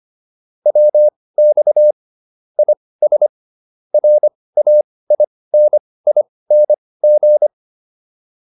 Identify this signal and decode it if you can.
Morse code, heard as a keyed tone, about 25 wpm, spaced as for 18 wpm, 600 Hz. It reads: WX IS RAINING